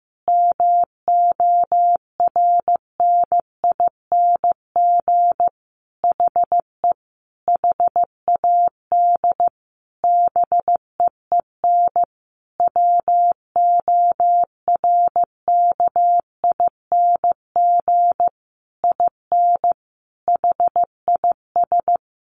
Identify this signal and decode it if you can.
Morse code, heard as a keyed tone, about 15 wpm, 700 Hz. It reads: MORNING HE HAD BEEN WORKING IN HIS
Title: Sentences from Top 1200 Words 15wpm